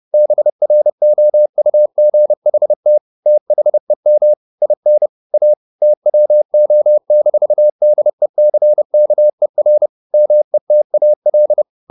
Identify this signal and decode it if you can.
Morse code, heard as a keyed tone, about 30 wpm, 600 Hz. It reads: BROUGHT THEM IN A TWO-DECKER METAL
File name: the old man and the sea / the old man and the sea 1 30wpm